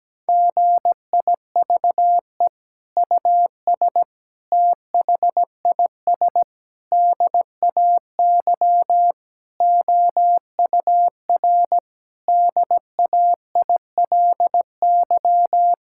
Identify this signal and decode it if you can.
Morse code, heard as a keyed tone, about 17 wpm, 700 Hz. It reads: GIVE US THIS DAY OUR DAILY
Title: Sentences from Top 1600 Words 17wpm